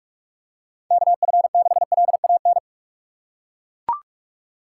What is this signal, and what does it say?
Morse code, heard as a keyed tone, about 42 wpm, 700 Hz. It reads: KF6LAN
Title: USA Call Signs - 6 Characters - 3x Character Spacing - 45wpm